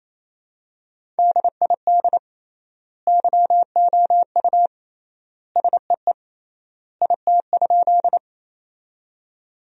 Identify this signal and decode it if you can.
Morse code, heard as a keyed tone, about 28 wpm, 700 Hz. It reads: DID YOU SEE IT?
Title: 3x Word-Spacing - Sentences from Top 100 Words - 28wpm